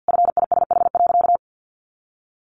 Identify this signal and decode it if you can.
Morse code, heard as a keyed tone, about 50 wpm, 700 Hz. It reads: FISH.
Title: Sentences from Top 1400 Words 50wpm